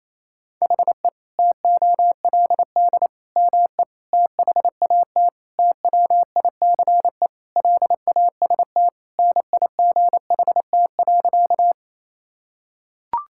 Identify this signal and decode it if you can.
Morse code, heard as a keyed tone, about 28 wpm, 700 Hz. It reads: HE TOLD ME THAT TWICE LAST NIGHT.
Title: Sentences from Top 1600 Words 28wpm